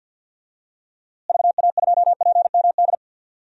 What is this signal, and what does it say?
Morse code, heard as a keyed tone, about 52 wpm, 700 Hz. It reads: VA2PMD
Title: DXCC-100 Call Signs - 6 Characters - Repeated 3x - 50wpm